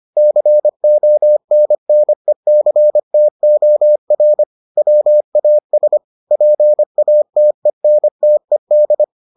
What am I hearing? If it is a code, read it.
Morse code, heard as a keyed tone, about 25 wpm, 600 Hz. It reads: CONNECTOR WAS PATENTED